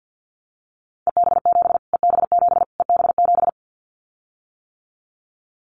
Morse code, heard as a keyed tone, about 49 wpm, 700 Hz. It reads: EB7 EB7 EB7
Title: Callsign Prefixes for Top 25 DXCC - Repeated 3x - 50wpm